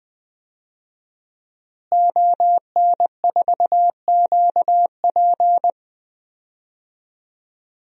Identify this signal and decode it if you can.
Morse code, heard as a keyed tone, about 21 wpm, 700 Hz. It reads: ON4QP
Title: DXCC-100 Call Signs - 5 Characters - Repeated 2x - 20wpm